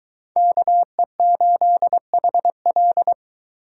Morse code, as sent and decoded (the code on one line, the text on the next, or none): -.- . ---.. .... .-..
KE8HL